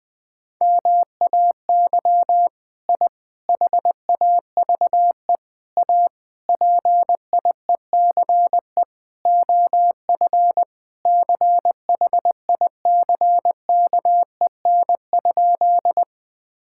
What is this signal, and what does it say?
Morse code, heard as a keyed tone, about 20 wpm, 700 Hz. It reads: MAY I HAVE A PIECE OF CHICKEN?
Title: Sentences from Top 1000 Words - Review Part 5 - 20wpm